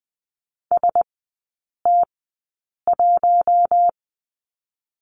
Morse code, heard as a keyed tone, about 20 wpm, spaced as for 8 wpm, 700 Hz. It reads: ST1
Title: 067 Lesson 12e - Letter d - Three Characters 20wpm